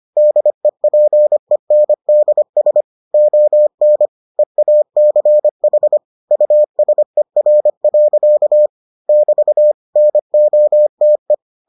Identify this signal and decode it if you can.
Morse code, heard as a keyed tone, about 25 wpm, 600 Hz. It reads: DEPENDS ON EACH USER. <BT> NOTE